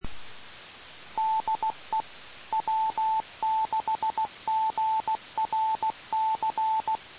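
Morse code, heard as a keyed tone, about 16 wpm, 880 Hz. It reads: DE W6GRC
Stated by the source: Morse recorded on the 2 m band, 147.105 MHz.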